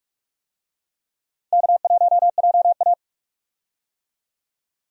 Morse code, heard as a keyed tone, about 43 wpm, 700 Hz. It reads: K1JA
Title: USA Call Signs - 4 Characters - 45wpm